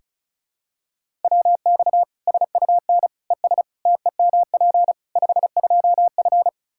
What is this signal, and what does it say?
Morse code, heard as a keyed tone, about 34 wpm, 700 Hz. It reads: WX SUN ES TEMP 52F
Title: Practice QSO - Exchange 2 - Info - 35wpm